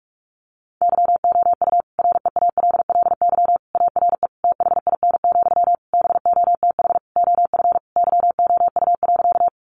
Morse code, heard as a keyed tone, about 44 wpm, 700 Hz. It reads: YOU REALLY ARE THIN, BOTH OF YOU.
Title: Sentences from Top 1200 Words 45wpm